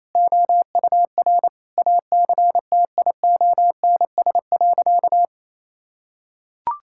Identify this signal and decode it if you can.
Morse code, heard as a keyed tone, about 28 wpm, 690 Hz. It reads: OUR ACTIONS.
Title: Sentences from Top 1000 Words - Review Part 6 - 28wpm